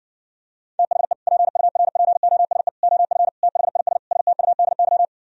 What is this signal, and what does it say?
Morse code, heard as a keyed tone, about 60 wpm, 700 Hz. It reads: THE PURPOSE OF THIS STUDY